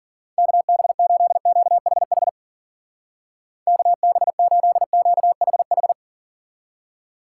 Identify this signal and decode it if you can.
Morse code, heard as a keyed tone, about 48 wpm, 700 Hz. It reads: KB8QHH KB8QHH
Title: Speed Racing - USA 2x3 Call Signs 40wpm